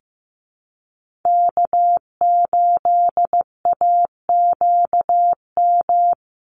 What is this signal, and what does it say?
Morse code, heard as a keyed tone, about 15 wpm, 700 Hz. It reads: K8AQM
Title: Call Signs Repeated 3x - 15wpm